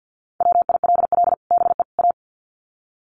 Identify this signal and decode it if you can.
Morse code, heard as a keyed tone, about 50 wpm, 700 Hz. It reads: WILL BE A
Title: Sets of 3 Words 50wpm